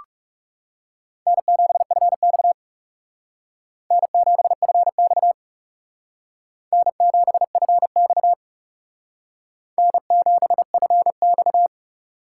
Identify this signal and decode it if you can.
Morse code, heard as a keyed tone, about 37 wpm, 700 Hz. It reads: N7FX N7FX N7FX N7FX
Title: Speed Racing - USA Call Signs 30wpm